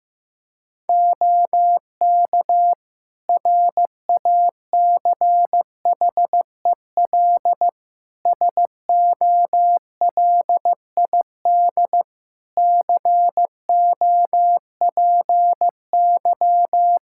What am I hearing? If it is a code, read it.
Morse code, heard as a keyed tone, about 15 wpm, 700 Hz. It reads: OK RACHEL SOLID COPY
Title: K1USN SST QSO Practice 15wpm